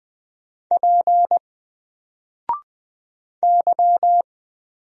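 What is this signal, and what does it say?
Morse code, heard as a keyed tone, about 20 wpm, 700 Hz. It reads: P Y
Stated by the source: Single letter and number Morse drill at 20 wpm, from N4LCD.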